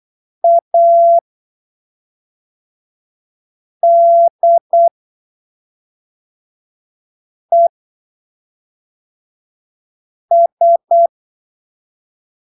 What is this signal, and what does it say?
Morse code, heard as a keyed tone, about 8 wpm, spaced as for 3 wpm, 680 Hz. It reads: ADES
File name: arquivos / texto / texto 23 08ppm